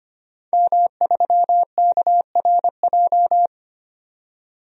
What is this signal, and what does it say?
Morse code, heard as a keyed tone, about 25 wpm, 700 Hz. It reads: M3KRJ
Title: DXCC-100 Call Signs - 5 Characters - 25wpm